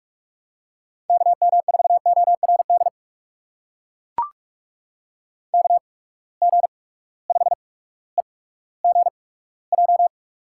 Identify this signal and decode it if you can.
Morse code, heard as a keyed tone, about 44 wpm, 690 Hz. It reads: K M 4 O R D K G 5 E G J
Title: USA Call Signs - 8x Character Spacing - 45wpm